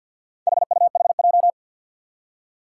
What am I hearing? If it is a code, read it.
Morse code, heard as a keyed tone, about 52 wpm, 700 Hz. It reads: HUHJ
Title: Four Challenging Letter-Number Combinations 50wpm